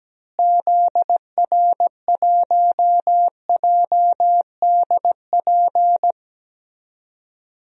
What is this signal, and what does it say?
Morse code, heard as a keyed tone, about 17 wpm, 700 Hz. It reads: ZR1JDP
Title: DXCC-100 Call Signs - 6 Characters - Repeated 2x - 17wpm